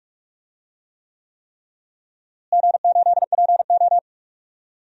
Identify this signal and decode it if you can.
Morse code, heard as a keyed tone, about 46 wpm, 700 Hz. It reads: G8PO
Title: DXCC-100 Call Signs - 4 Characters - Repeated 3x - 45wpm